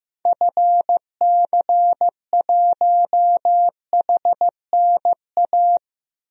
Morse code, heard as a keyed tone, about 15 wpm, 700 Hz. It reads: FC1HNA
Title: DXCC-100 Call Signs - 6 Characters - 15wpm